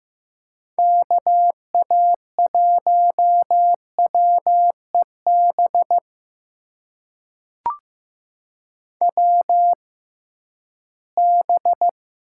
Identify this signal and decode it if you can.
Morse code, heard as a keyed tone, about 15 wpm, spaced as for 11 wpm, 700 Hz. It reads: KA1WEB WB
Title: USA Call Signs - 6 Characters - 6x Character Spacing - 15wpm